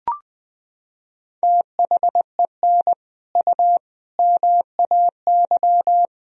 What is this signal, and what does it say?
Morse code, heard as a keyed tone, about 20 wpm, 700 Hz. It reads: THEN U MAY
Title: Sentences from Top 100 Words and Abbreviations 20wpm